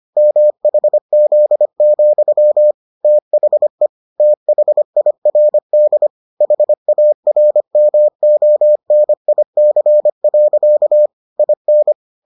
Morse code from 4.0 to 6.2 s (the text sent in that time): THIRD